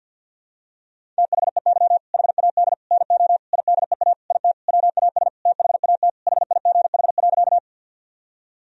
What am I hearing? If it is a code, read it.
Morse code, heard as a keyed tone, about 52 wpm, 700 Hz. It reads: THEY HAD NO IDEA IT WAS THAT HIGH.
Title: Sentences from Top 1000 Words - Review Part 1 - 50wpm